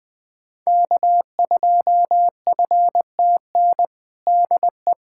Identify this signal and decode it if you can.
Morse code, heard as a keyed tone, about 20 wpm, 700 Hz. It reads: K2FTN DE
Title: USA Callsign DE Callsign 20wpm